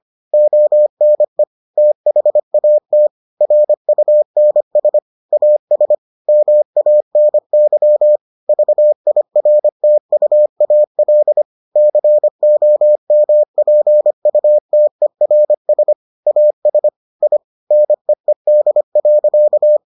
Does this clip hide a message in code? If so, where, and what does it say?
Yes: Morse code from 0.3 to 19.8 s, heard as a keyed tone, about 25 wpm, 600 Hz. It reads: ONE THAT RUNS AS MANY VIRTUAL COMPUTERS AS I NEED.